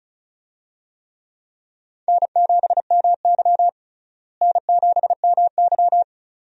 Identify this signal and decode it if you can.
Morse code, heard as a keyed tone, about 36 wpm, 700 Hz. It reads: N7MY N7MY
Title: USA Call Signs - 4 Characters - Repeated 2x - 35wpm